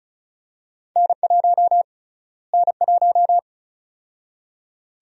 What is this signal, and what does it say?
Morse code, heard as a keyed tone, about 36 wpm, 700 Hz. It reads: N1 N1
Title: USA Call Sign Prefixes - Repeated 2x - 35wpm